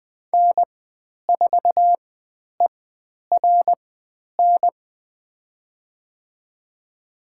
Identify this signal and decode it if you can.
Morse code, heard as a keyed tone, about 20 wpm, spaced as for 10 wpm, 700 Hz. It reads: N4ERN
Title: 109 Lesson 18g - Letter m - Call Signs 20wpm